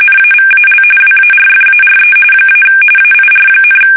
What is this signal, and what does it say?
RTTY, about 45.45 baud. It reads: NAVAL WARFARE - AIR,
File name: fsk850 / battleshipnj